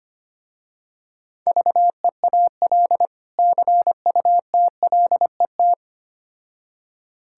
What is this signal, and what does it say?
Morse code, heard as a keyed tone, about 25 wpm, 700 Hz. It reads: VEAL CUTLET
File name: practice / 25wpm / Difficult Words with Hints - 25wpm